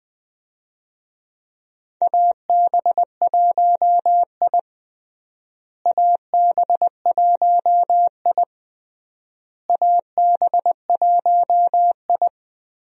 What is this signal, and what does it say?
Morse code, heard as a keyed tone, about 20 wpm, 700 Hz. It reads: AB1I AB1I AB1I
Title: USA Call Signs - 4 Characters - Repeated 3x - 20wpm